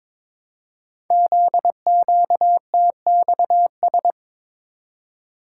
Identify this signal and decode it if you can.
Morse code, heard as a keyed tone, about 23 wpm, 700 Hz. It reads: ZQTXS
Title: Code Groups - 5 Letters - 22wpm